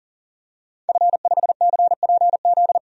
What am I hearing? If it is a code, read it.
Morse code, heard as a keyed tone, about 40 wpm, 700 Hz. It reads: F5CPZ
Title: DXCC-100 Call Signs - 5 Characters - 40wpm